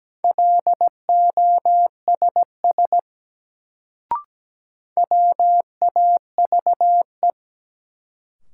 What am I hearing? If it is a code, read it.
Morse code, heard as a keyed tone, about 17 wpm, 700 Hz. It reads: LOSS WAVE
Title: Sentences from Top 1400 Words - New Word Drill - 17wpm